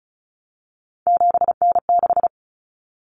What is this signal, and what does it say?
Morse code, heard as a keyed tone, about 34 wpm, 700 Hz. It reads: 7N6
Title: Common Cut Numbers - 3-Digit 35wpm